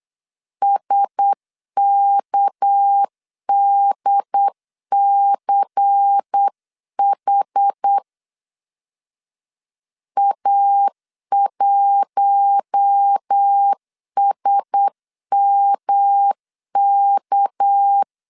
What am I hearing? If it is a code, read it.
Morse code, heard as a keyed tone, about 8 wpm, 800 Hz. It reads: SKDCH A1SMK
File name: 8wpm